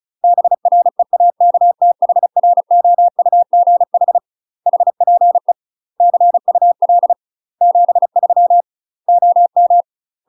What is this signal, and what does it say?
Morse code, heard as a keyed tone, about 36 wpm, 700 Hz. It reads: BREAKTHROUGH HPE CUL 73 OM
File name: UK MASTER QSOs TRAINING / 35WPM / UK MASTER QSOs 0026